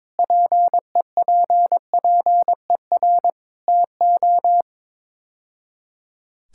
Morse code, heard as a keyed tone, about 22 wpm, 700 Hz. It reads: PEPPER TO
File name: practice / 22wpm / Build A Phrase - 5-Word Phrases - 22wpm